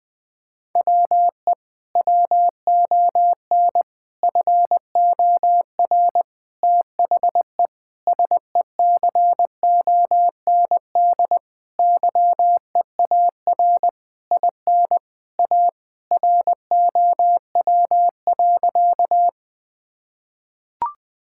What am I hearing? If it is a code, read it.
Morse code, heard as a keyed tone, about 20 wpm, 700 Hz. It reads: WE WON FOR THE SECOND YEAR IN A ROW.
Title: Sentences with 6-Word Phrases - 20wpm